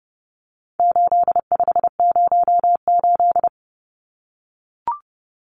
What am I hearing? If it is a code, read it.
Morse code, heard as a keyed tone, about 30 wpm, 700 Hz. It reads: 8508
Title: Numbers - 4-Digit 30wpm